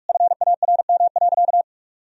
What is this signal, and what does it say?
Morse code, heard as a keyed tone, about 47 wpm, 700 Hz. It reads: FARM.
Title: Sentences from Top 1000 Words - Review Part 2 - 45wpm